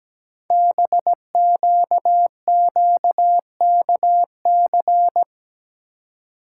Morse code, heard as a keyed tone, about 17 wpm, 700 Hz. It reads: BQQKC
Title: Code Groups - 5 Letters - 17wpm